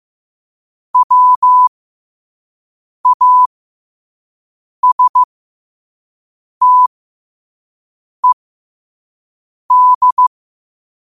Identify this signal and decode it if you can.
Morse code, heard as a keyed tone, about 15 wpm, spaced as for 5 wpm, 1000 Hz. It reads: WASTED